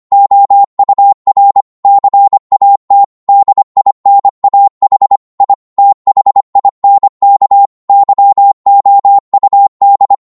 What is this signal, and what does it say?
Morse code, heard as a keyed tone, about 25 wpm, 800 Hz. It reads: OUR CAT DINAH I THINK YOUD